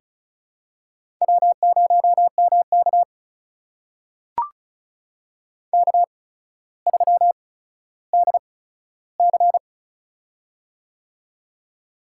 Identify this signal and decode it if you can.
Morse code, heard as a keyed tone, about 36 wpm, spaced as for 15 wpm, 700 Hz. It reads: W0MK K3DC
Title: USA Call Signs - 4 Characters - 8x Character Spacing - 35wpm